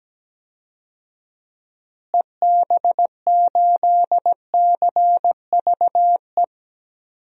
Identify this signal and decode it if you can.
Morse code, heard as a keyed tone, about 17 wpm, 700 Hz. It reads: EB8CVE
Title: DXCC-100 Call Signs - 6 Characters - Repeated 3x - 17wpm